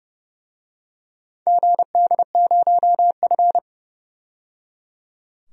Morse code, heard as a keyed tone, about 30 wpm, 700 Hz. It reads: GD0F
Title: DXCC-100 Call Signs - 4 Characters - 30wpm